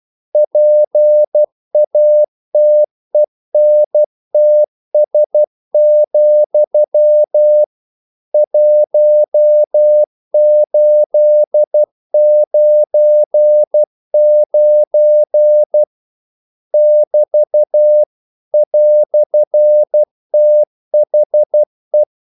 Morse code, heard as a keyed tone, about 12 wpm, 600 Hz. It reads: PATENTS, 1899 <BT> "THE
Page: English quotations in CW (telegraphy - morse)